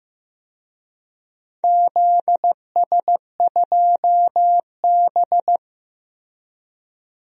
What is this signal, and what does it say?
Morse code, heard as a keyed tone, about 16 wpm, 700 Hz. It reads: ZS2B
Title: DXCC-100 Call Signs - 4 Characters - Repeated 3x - 15wpm